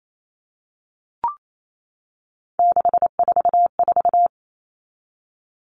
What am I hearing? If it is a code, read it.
Morse code, heard as a keyed tone, about 28 wpm, 700 Hz. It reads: 644
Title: Numbers - 3-Digit 28wpm